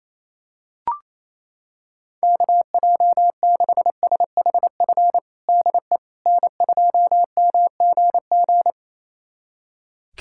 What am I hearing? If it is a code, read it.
Morse code, heard as a keyed tone, about 28 wpm, 700 Hz. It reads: KJ6SHF DE N2MGG
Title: USA Callsign DE Callsign 28wpm